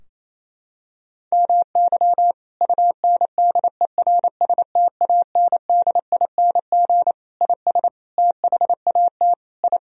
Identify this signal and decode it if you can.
Morse code, heard as a keyed tone, about 27 wpm, 700 Hz. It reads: MY UNDERSTANDING IS THAT I
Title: Sentences from Top 1000 Words - Review Part 3 - 28wpm